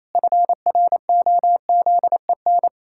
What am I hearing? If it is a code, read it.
Morse code, heard as a keyed tone, about 28 wpm, 700 Hz. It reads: FROZEN
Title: Pangram Challenge v2 - 28wpm